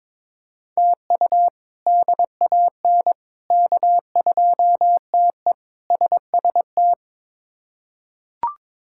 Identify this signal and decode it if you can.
Morse code, heard as a keyed tone, about 21 wpm, 700 Hz. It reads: TU DAN K2TE SST
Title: K1USN SST QSO Practice - 22wpm